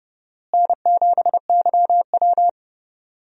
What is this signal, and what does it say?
Morse code, heard as a keyed tone, about 32 wpm, 700 Hz. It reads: N7YW
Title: USA Call Signs - 4 Characters - Repeated 3x - 30wpm